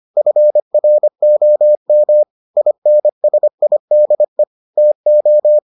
Morse code, heard as a keyed tone, about 25 wpm, 600 Hz. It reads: FROM INSIDE TO